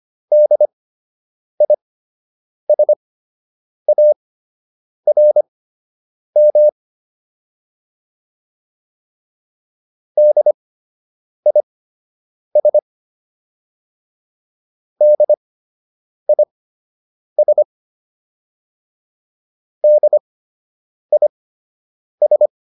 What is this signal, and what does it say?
Morse code, heard as a keyed tone, about 25 wpm, spaced as for 8 wpm, 600 Hz. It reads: DISARM DIS DIS DIS